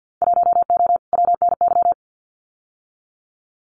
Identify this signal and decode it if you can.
Morse code, heard as a keyed tone, about 50 wpm, 700 Hz. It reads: 1O WNY
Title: Winter Field Day Exchanges 50wpm